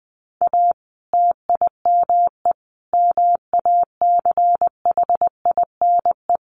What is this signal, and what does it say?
Morse code, heard as a keyed tone, about 20 wpm, 700 Hz. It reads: A TIME MACHINE
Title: A Sound Of Thunder 20wpm